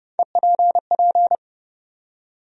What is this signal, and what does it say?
Morse code, heard as a keyed tone, about 30 wpm, 700 Hz. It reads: EPP
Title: USA Call Sign Postfixes - 3 Letters 30wpm